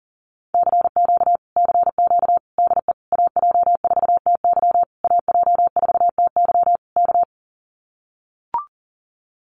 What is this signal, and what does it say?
Morse code, heard as a keyed tone, about 40 wpm, 700 Hz. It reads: CQ CQ DE AJ4TY AJ4TY K
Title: Common QSO Phrases Encore 40wpm